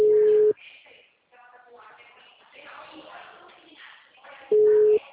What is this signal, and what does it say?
Ringback tone (Europe, 425 Hz), with speech.